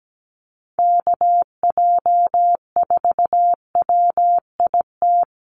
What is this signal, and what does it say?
Morse code, heard as a keyed tone, about 17 wpm, 700 Hz. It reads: KJ4WIT
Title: USA Call Signs Repeated 3x 17wpm